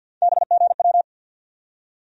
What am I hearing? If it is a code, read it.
Morse code, heard as a keyed tone, about 48 wpm, 700 Hz. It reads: BGW
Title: USA Call Sign Postfixes - 3 Letters 50wpm